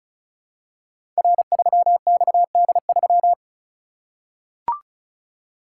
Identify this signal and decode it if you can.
Morse code, heard as a keyed tone, about 36 wpm, 700 Hz. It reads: R3XD3
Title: Code Groups - 5 Letter:Numbers - 35wpm